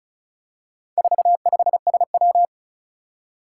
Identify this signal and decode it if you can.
Morse code, heard as a keyed tone, about 36 wpm, 700 Hz. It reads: 45SW
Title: Four Challenging Letter-Number Combinations 35wpm